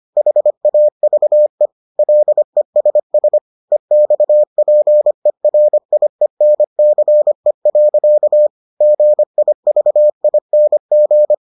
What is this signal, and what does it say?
Morse code, heard as a keyed tone, about 25 wpm, 600 Hz. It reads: HAVE LESS EXPERIENCE. GIVING